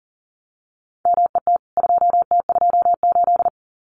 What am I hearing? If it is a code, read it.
Morse code, heard as a keyed tone, about 40 wpm, 700 Hz. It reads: MET 2T28
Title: CWT Exchange 40wpm